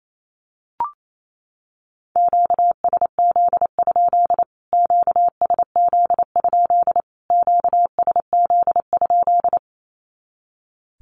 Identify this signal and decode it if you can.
Morse code, heard as a keyed tone, about 28 wpm, 700 Hz. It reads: QSZ? QSZ? QSZ?
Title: Q-Codes Repeated 3x 28wpm